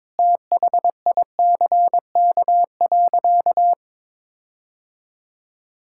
Morse code, heard as a keyed tone, about 22 wpm, 700 Hz. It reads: THICK.
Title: Sentences from Top 1600 Words 22wpm